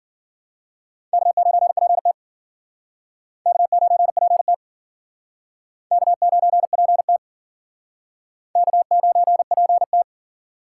Morse code, heard as a keyed tone, about 54 wpm, 700 Hz. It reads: K9PT K9PT K9PT K9PT